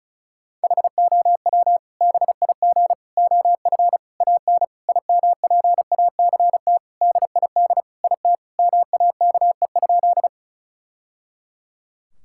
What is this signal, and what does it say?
Morse code, heard as a keyed tone, about 36 wpm, 700 Hz. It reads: HOW BIG OF AN IMPACT DID IT MAKE?
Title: Sentences from Top 1000 Words - Review Part 6 - 35wpm